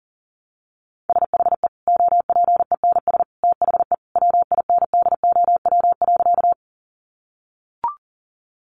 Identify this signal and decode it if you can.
Morse code, heard as a keyed tone, about 40 wpm, 700 Hz. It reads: SHE OPENS THE WINDOW.
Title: Easy Phrases 40wpm